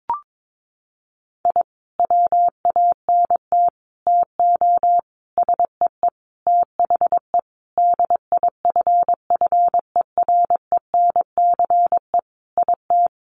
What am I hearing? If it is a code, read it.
Morse code, heard as a keyed tone, about 22 wpm, 700 Hz. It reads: I WANT TO SEE THE DIFFERENCE IT
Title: Sentences from Top 1300 Words 22wpm